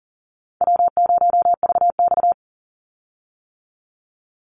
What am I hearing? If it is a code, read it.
Morse code, heard as a keyed tone, about 40 wpm, 700 Hz. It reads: W0VX
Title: Easy Call Signs - 40wpm